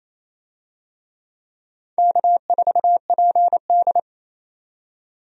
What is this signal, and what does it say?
Morse code, heard as a keyed tone, about 28 wpm, 700 Hz. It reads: K4PD
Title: USA Call Signs - 4 Characters - 28wpm